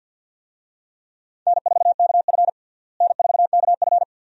Morse code, heard as a keyed tone, about 52 wpm, 700 Hz. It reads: N4KF N4KF
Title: USA Call Signs - 4 Characters - Repeated 2x - 50wpm